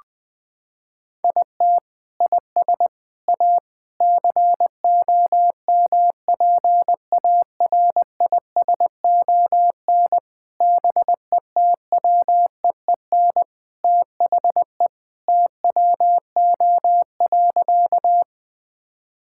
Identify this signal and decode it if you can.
Morse code, heard as a keyed tone, about 21 wpm, 700 Hz. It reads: IT IS A COMPARISON BETWEEN THE TWO.
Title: Sentences from Top 1600 Words 20wpm